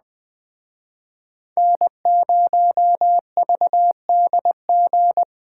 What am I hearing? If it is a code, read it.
Morse code, heard as a keyed tone, about 21 wpm, 700 Hz. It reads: N0VDG
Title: Speed Racing - USA 1x3 Call Signs 17wpm